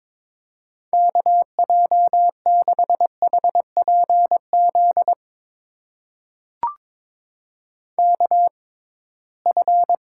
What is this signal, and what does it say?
Morse code, heard as a keyed tone, about 22 wpm, spaced as for 17 wpm, 700 Hz. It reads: KJ6HPZ KF
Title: USA Call Signs - 6 Characters - 6x Character Spacing - 22wpm